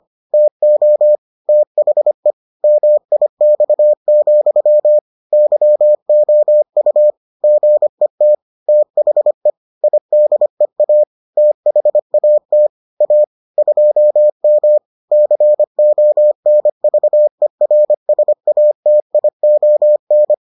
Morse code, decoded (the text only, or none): TO THE MIX, YOU GET THE IDEA THAT A 2M CONVERSATION